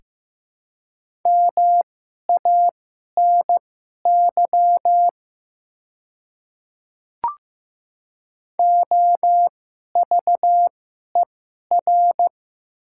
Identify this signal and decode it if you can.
Morse code, heard as a keyed tone, about 15 wpm, spaced as for 11 wpm, 690 Hz. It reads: MANY OVER
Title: 2x Character Spacing - Top 100 Words - 15wpm